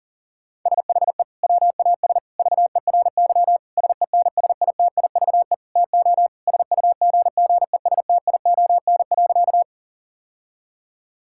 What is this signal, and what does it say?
Morse code, heard as a keyed tone, about 40 wpm, 700 Hz. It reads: SHE WAS VERY SENSITIVE TO SUGGESTION.
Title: Sentences from Top 1600 Words 40wpm